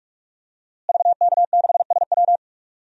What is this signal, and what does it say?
Morse code, heard as a keyed tone, about 44 wpm, 700 Hz. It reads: VK6SW
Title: DXCC-100 Call Signs - 5 Characters - Repeated 3x - 45wpm